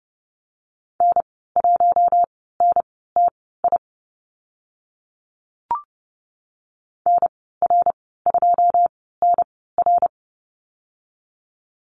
Morse code, heard as a keyed tone, about 30 wpm, spaced as for 17 wpm, 700 Hz. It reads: N1NTI NR2NR
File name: Lesson 14g - Number 2 - Call Signs 30wpm